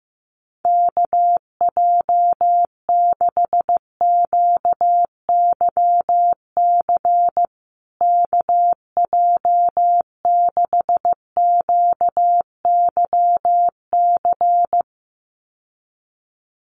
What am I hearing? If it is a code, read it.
Morse code, heard as a keyed tone, about 15 wpm, 700 Hz. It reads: KJ6QYC KJ6QYC
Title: USA Call Signs Repeated 2x 15wpm